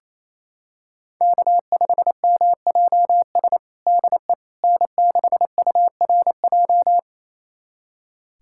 Morse code, heard as a keyed tone, about 27 wpm, 700 Hz. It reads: K5MJS DE N6URJ
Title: USA Callsign DE Callsign - 5 Characters - 28wpm